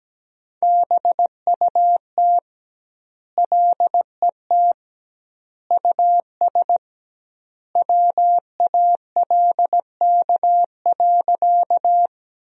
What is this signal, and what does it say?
Morse code, heard as a keyed tone, about 17 wpm, 700 Hz. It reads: BUT LET US WALK.
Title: 2x Word-Spacing - Sentences from Top 300 Words - 17wpm